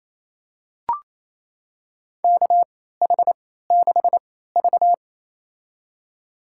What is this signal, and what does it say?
Morse code, heard as a keyed tone, about 28 wpm, spaced as for 16 wpm, 700 Hz. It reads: KH6V
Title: USA Call Signs - 3x Character Spacing - 28wpm